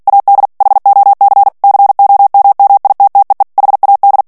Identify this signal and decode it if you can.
Morse code, heard as a keyed tone, about 48 wpm, 800 Hz. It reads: AD HOC COMMITTEE HAD
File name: ANGLAIS texte clair 3 vitesse 45 mots minute